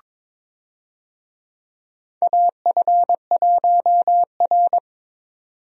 Morse code, heard as a keyed tone, about 22 wpm, 700 Hz. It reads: AF1R
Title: USA Call Signs - 4 Characters - 22wpm